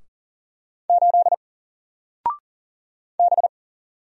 Morse code, heard as a keyed tone, about 40 wpm, 700 Hz. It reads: 8 B
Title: Single Letter-Number Alt 40wpm